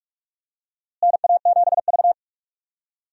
Morse code, heard as a keyed tone, about 43 wpm, 700 Hz. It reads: NA7V
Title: USA Call Signs - 4 Characters - Repeated 2x - 45wpm